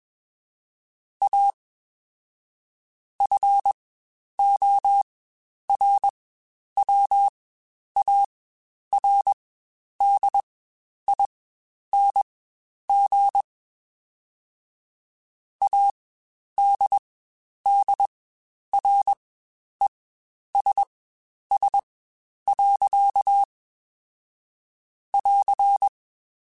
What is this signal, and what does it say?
Morse code, heard as a keyed tone, about 21 wpm, spaced as for 10 wpm, 800 Hz. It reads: A FORWARDING ADDRESS. <AR>
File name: Spaced / Chuckles Spaced-mp3s / Temptation sp @20